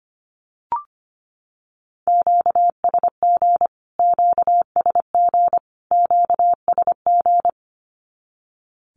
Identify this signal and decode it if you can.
Morse code, heard as a keyed tone, about 25 wpm, 700 Hz. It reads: QSG QSG QSG
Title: Q-Codes Repeated 3x 25wpm